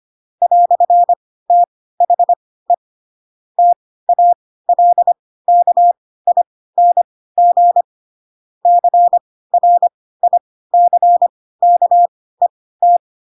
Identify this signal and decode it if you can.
Morse code, heard as a keyed tone, about 25 wpm, spaced as for 16 wpm, 700 Hz. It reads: "THE TALKING CRICKET